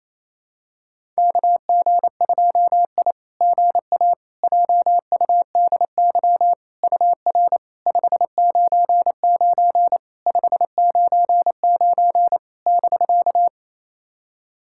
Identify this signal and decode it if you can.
Morse code, heard as a keyed tone, about 28 wpm, 690 Hz. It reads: KG2I GA JUDY UR 599 599 <BK>
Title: SOTA QSO Practice 28wpm